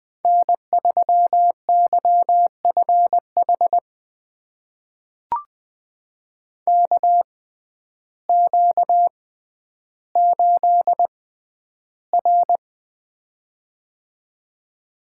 Morse code, heard as a keyed tone, about 20 wpm, spaced as for 11 wpm, 700 Hz. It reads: N3YFH KQ8R
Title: USA Call Signs - 6x Character Spacing - 20wpm